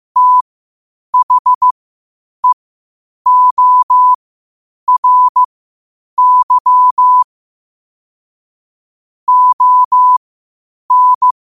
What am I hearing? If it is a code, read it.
Morse code, heard as a keyed tone, about 15 wpm, spaced as for 8 wpm, 1000 Hz. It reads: THEORY ON
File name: qotdfiles / 8wpm